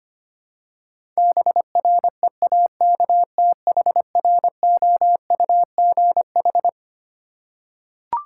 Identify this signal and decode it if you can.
Morse code, heard as a keyed tone, about 25 wpm, 700 Hz. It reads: BREAKTHROUGH
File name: Twelve Letter Words - 25wpm